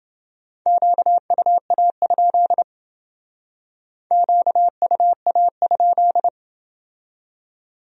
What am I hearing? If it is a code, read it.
Morse code, heard as a keyed tone, about 29 wpm, 700 Hz. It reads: QUA? QUA?